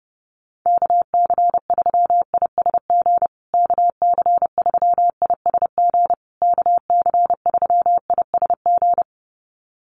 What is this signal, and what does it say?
Morse code, heard as a keyed tone, about 30 wpm, 700 Hz. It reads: KC3ISG KC3ISG KC3ISG
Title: USA Call Signs Repeated 3x 30wpm